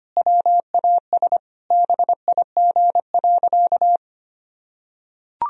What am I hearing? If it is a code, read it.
Morse code, heard as a keyed tone, about 25 wpm, 700 Hz. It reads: WAS BIG.
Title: Sentences from Top 1000 Words - Review Part 2 - 25wpm